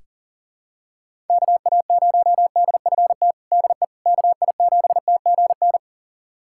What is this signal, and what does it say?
Morse code, heard as a keyed tone, about 37 wpm, 700 Hz. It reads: KA0DFT DE KI7TGN
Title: USA Callsign DE Callsign - 6 Characters - 40wpm